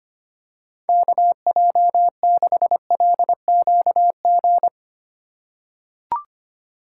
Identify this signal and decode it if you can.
Morse code, heard as a keyed tone, about 25 wpm, 700 Hz. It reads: KJ6LQG
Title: USA Call Signs - 6 Characters - 3x Character Spacing - 25wpm